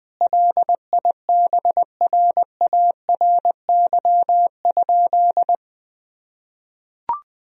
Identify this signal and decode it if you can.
Morse code, heard as a keyed tone, about 20 wpm, 700 Hz. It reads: LIBRARY?
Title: Sentences from Top 1000 Words - Review Part 4 - 20wpm